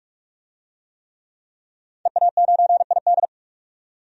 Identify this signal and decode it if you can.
Morse code, heard as a keyed tone, about 42 wpm, 700 Hz. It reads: EA9ID
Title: DXCC-100 Call Signs - 5 Characters - 45wpm